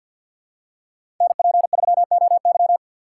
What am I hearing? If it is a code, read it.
Morse code, heard as a keyed tone, about 47 wpm, 700 Hz. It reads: NP3OY
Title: USA Call Signs - 3x Character Spacing - 50wpm